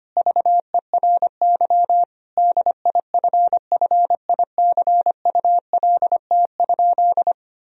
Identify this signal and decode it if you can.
Morse code, heard as a keyed tone, about 25 wpm, 700 Hz. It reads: VERY DIFFICULT?
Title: Sentences from Top 1000 Words - Review Part 4 - 25wpm